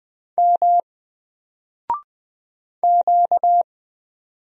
Morse code, heard as a keyed tone, about 20 wpm, 700 Hz. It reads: M Q
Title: Single Letter-Number Alt 20wpm